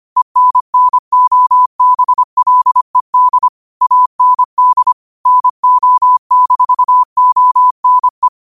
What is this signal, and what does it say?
Morse code, heard as a keyed tone, about 25 wpm, 1000 Hz. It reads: ENNOBLED AND NO-ONE